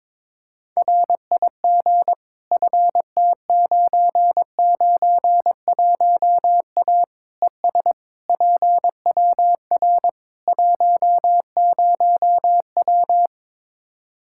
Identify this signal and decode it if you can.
Morse code, heard as a keyed tone, about 21 wpm, 700 Hz. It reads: RIG FT991A ES PWR 10W
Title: Practice QSO - Exchange 2 - Info - 22wpm